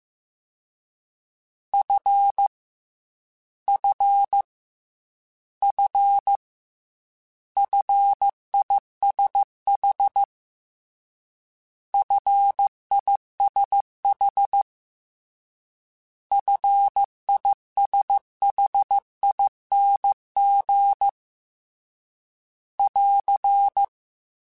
Morse code, heard as a keyed tone, about 15 wpm, 800 Hz. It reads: F F F FISH FISH FISHING <AR>
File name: Short Runs / Short Runs-mp3s / SinglePrefixes @13